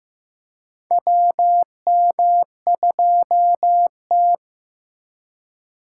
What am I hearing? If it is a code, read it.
Morse code, heard as a keyed tone, about 15 wpm, 700 Hz. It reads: WM2T
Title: USA Call Signs - 4 Characters - 4x Character Spacing - 15wpm